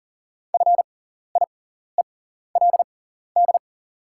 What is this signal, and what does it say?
Morse code, heard as a keyed tone, about 40 wpm, spaced as for 14 wpm, 700 Hz. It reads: FIELD
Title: 6x Character Spacing - Top 500 Words - 40wpm